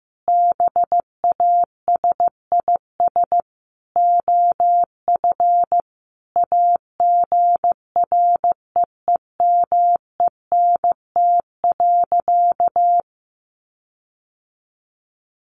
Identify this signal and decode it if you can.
Morse code, heard as a keyed tone, about 15 wpm, 700 Hz. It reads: BASIS OF AGREEMENT.
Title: Sentences from Top 1500 Words 15wpm